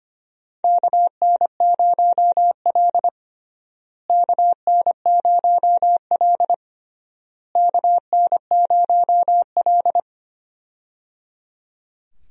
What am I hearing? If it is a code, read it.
Morse code, heard as a keyed tone, about 25 wpm, 700 Hz. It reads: KN0L KN0L KN0L
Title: USA Call Signs - 4 Characters - Repeated 3x - 25wpm